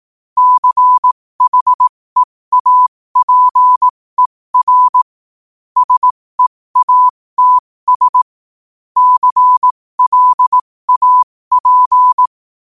Morse code, heard as a keyed tone, about 18 wpm, spaced as for 15 wpm, 1000 Hz. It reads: CHEAPER SEATS CLAP
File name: qotdfiles / 15wpm